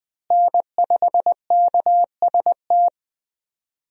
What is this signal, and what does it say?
Morse code, heard as a keyed tone, about 21 wpm, 700 Hz. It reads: N5KST
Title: USA Call Signs - 5 Characters - Repeated 2x - 20wpm